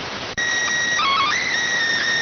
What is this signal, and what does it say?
SSTV (slow-scan television), PD 290.